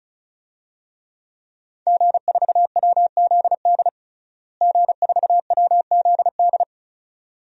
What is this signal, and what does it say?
Morse code, heard as a keyed tone, about 36 wpm, 700 Hz. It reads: G4WZD G4WZD
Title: DXCC-100 Call Signs - 5 Characters - Repeated 2x - 35wpm